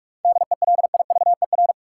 Morse code, heard as a keyed tone, about 47 wpm, 700 Hz. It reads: DELIVER
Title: Seven Letter Words - 45wpm